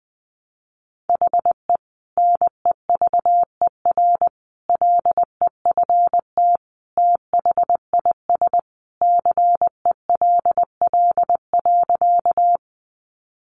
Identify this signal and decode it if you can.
Morse code, heard as a keyed tone, about 19 wpm, 700 Hz. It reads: HE NEVER LEFT THIS CELL.
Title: Sentences from Top 1100 Words 20wpm